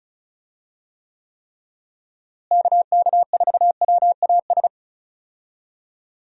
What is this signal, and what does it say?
Morse code, heard as a keyed tone, about 36 wpm, 700 Hz. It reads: KK4WAS
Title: USA Call Signs - 6 Characters - Repeated 3x - 35wpm